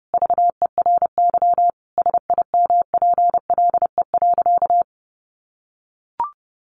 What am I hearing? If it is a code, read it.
Morse code, heard as a keyed tone, about 28 wpm, 700 Hz. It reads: VERY SIMPLE.
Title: Sentences from Top 500 Words 30wpm